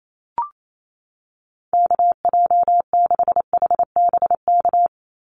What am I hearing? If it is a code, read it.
Morse code, heard as a keyed tone, about 28 wpm, 700 Hz. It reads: KJ6HBK
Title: USA Call Signs 28wpm